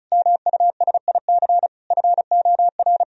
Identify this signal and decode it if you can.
Morse code, heard as a keyed tone, about 36 wpm, 700 Hz. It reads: MUSIC FOR
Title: Sentences from Top 1000 Words - Review Part 1 - 35wpm